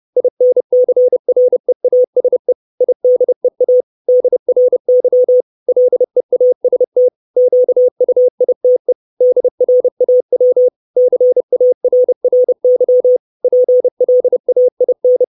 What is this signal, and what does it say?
Morse code, heard as a keyed tone, about 30 wpm, 500 Hz. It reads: INCREASE IDEA DRY LEAST QUITE DRAW CARRY PLAIN
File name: 632 Words 30WPM